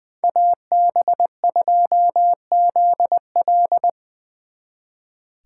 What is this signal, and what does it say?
Morse code, heard as a keyed tone, about 25 wpm, 700 Hz. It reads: AB2ZL
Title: Speed Racing - USA 2x2 Call Signs 20wpm